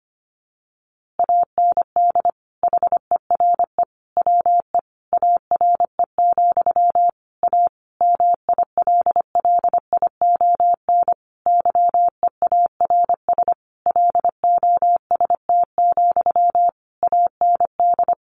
Morse code, heard as a keyed tone, about 25 wpm, 700 Hz. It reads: AND HERE WE ARE, A MILLION YEARS LOST, AND
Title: A Sound Of Thunder 25wpm